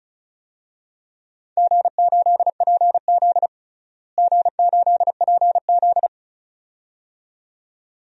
Morse code, heard as a keyed tone, about 36 wpm, 700 Hz. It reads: G8PZ G8PZ
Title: DXCC-100 Call Signs - 4 Characters - Repeated 2x - 35wpm